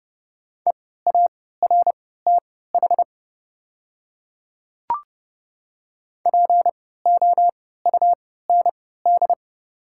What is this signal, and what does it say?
Morse code, heard as a keyed tone, about 30 wpm, spaced as for 17 wpm, 700 Hz. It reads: EARTH POUND
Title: 3x Character Spacing - Top 500 Words - 30wpm